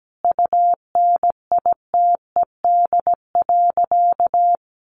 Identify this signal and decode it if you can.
Morse code, heard as a keyed tone, about 17 wpm, 700 Hz. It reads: UNITED.
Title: Sentences from Top 600 Words 17wpm